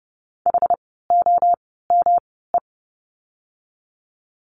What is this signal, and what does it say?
Morse code, heard as a keyed tone, about 30 wpm, spaced as for 17 wpm, 700 Hz. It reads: HOME
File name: course / 30:10wpm / 108 Lesson 18f - Letter m - Words 30wpm